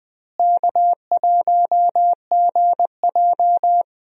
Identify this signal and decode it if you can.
Morse code, heard as a keyed tone, about 20 wpm, 700 Hz. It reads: K1GJ
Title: USA Call Signs - 4 Characters - Repeated 2x - 20wpm